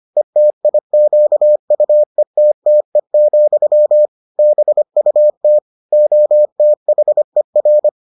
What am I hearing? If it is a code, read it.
Morse code, heard as a keyed tone, about 25 wpm, 600 Hz. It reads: ETIQUETTE, BUT OTHER